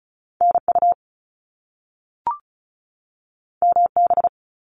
Morse code, heard as a keyed tone, about 36 wpm, 700 Hz. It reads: NU MB
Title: Canadian Provinces and Territories 35wpm